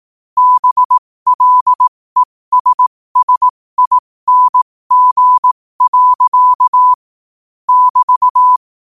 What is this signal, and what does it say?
Morse code, heard as a keyed tone, about 18 wpm, spaced as for 15 wpm, 1000 Hz. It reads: BLESSING. <BT>
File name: qotdfiles / 15wpm